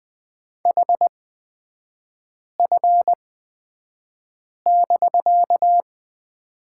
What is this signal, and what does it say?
Morse code, heard as a keyed tone, about 20 wpm, spaced as for 5 wpm, 700 Hz. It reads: HF<BK>
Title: 250 Lesson 40e - BK - Three Characters 20wpm